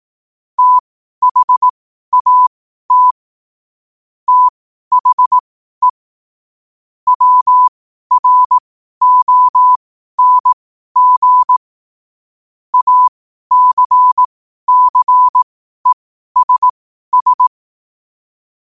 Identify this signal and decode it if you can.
Morse code, heard as a keyed tone, about 18 wpm, spaced as for 12 wpm, 1000 Hz. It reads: THAT THE WRONG ACCESS
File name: qotdfiles / 12wpm